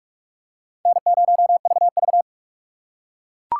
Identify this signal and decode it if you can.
Morse code, heard as a keyed tone, about 42 wpm, 700 Hz. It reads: N0VV